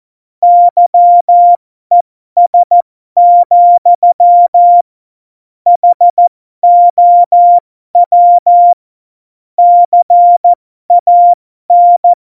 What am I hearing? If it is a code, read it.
Morse code, heard as a keyed tone, about 14 wpm, spaced as for 12 wpm, 700 Hz. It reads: YES, HOW CAN
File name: Morsekurs / Witze